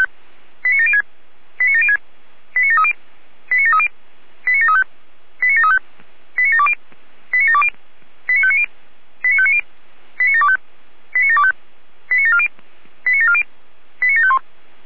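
Five-tone selective call (ZVEI), heard as quick runs of five tones, tones of about 70 ms. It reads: 89706 89706 89730 89730 89735 89735 89720 89720 89590 89590 89725 89725 89740 89740 89751
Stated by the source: ZVEI five-tone call received with an SDR.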